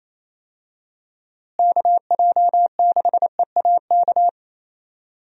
Morse code, heard as a keyed tone, about 28 wpm, 700 Hz. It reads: KJ6EAK
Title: USA Call Signs - 6 Characters - Repeated 2x - 28wpm